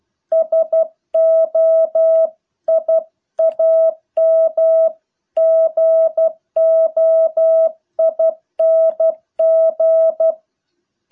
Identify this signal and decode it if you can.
Morse code, heard as a keyed tone, about 12 wpm, 650 Hz. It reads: SO I AM GOING